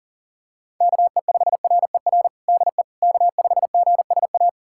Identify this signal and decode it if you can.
Morse code, heard as a keyed tone, about 37 wpm, 700 Hz. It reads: KE5RER DE K5GSA
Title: USA Callsign DE Callsign 40wpm